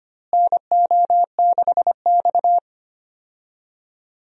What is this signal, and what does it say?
Morse code, heard as a keyed tone, about 25 wpm, 700 Hz. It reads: NO6X
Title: USA Call Signs - 4 Characters - Repeated 3x - 25wpm